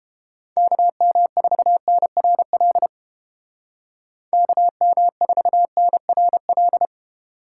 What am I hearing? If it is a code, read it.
Morse code, heard as a keyed tone, about 32 wpm, 700 Hz. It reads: KM4NRL KM4NRL